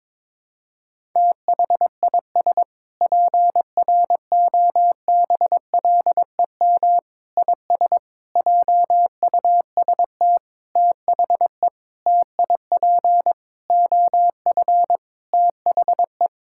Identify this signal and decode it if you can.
Morse code, heard as a keyed tone, about 21 wpm, 700 Hz. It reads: THIS PROBLEM IS JUST THE TIP OF THE
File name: Sentences with 7-Word Phrases - 22wpm